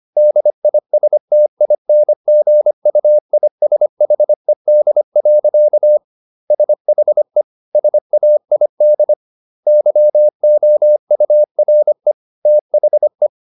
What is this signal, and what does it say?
Morse code, heard as a keyed tone, about 25 wpm, 600 Hz. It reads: DISTINGUISHED. SHE SAID YOURE THE